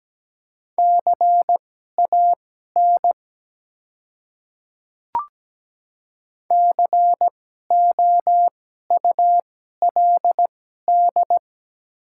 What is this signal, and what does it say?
Morse code, heard as a keyed tone, about 17 wpm, spaced as for 12 wpm, 700 Hz. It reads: CAN COULD
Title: 2x Character Spacing - Top 100 Words - 17wpm